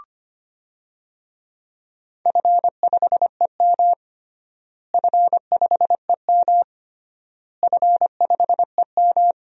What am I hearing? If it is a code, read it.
Morse code, heard as a keyed tone, about 25 wpm, 700 Hz. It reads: F5EM F5EM F5EM
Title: DXCC-100 Call Signs - 4 Characters - Repeated 3x - 25wpm